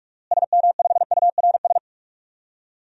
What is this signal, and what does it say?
Morse code, heard as a keyed tone, about 44 wpm, 700 Hz. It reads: SM5URS
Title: DXCC-100 Call Signs - 6 Characters - Repeated 3x - 45wpm